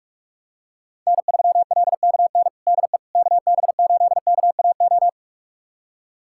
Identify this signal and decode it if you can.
Morse code, heard as a keyed tone, about 43 wpm, 700 Hz. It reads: N3LKN DE KB8KAO
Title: USA Callsign DE Callsign 45wpm